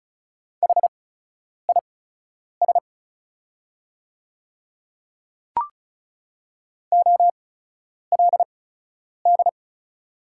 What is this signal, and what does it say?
Morse code, heard as a keyed tone, about 36 wpm, spaced as for 10 wpm, 690 Hz. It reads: HIS OLD
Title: 8x Character Spacing - Top 200 Words - 35wpm